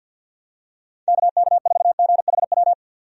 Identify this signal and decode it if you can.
Morse code, heard as a keyed tone, about 46 wpm, 700 Hz. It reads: KK4GHW
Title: USA Call Signs - 6 Characters - 50wpm